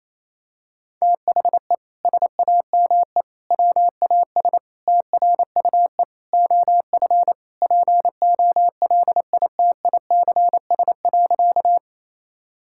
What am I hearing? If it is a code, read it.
Morse code, heard as a keyed tone, about 28 wpm, 700 Hz. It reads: THE SAME WAS TRUE OF POLITICS.
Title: Sentences from Top 1000 Words - Review Part 6 - 28wpm